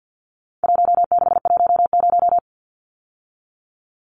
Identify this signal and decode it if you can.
Morse code, heard as a keyed tone, about 50 wpm, 700 Hz. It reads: 1610
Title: Numbers - 4-Digit 50wpm